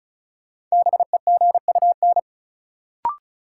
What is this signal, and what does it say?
Morse code, heard as a keyed tone, about 36 wpm, 700 Hz. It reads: BEGUN
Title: Sentences from Top 1000 Words - Review Part 6 - New Word Drill - 35wpm